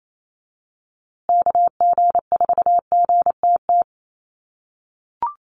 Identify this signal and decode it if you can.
Morse code, heard as a keyed tone, about 28 wpm, 700 Hz. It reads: KG4GTT
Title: USA Call Signs Repeated 2x 28wpm